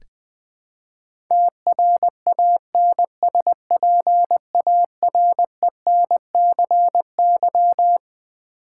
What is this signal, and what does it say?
Morse code, heard as a keyed tone, about 20 wpm, 700 Hz. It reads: TRANSPARENCY
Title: Build A Word - 5 Letters At A Time - 12-Letter Words - 20wpm